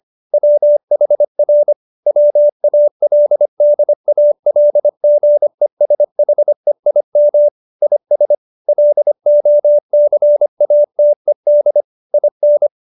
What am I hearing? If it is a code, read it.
Morse code, heard as a keyed tone, about 25 wpm, 590 Hz. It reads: WHR WALDALGESHEIM IS LOCATED IN
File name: QSO318 25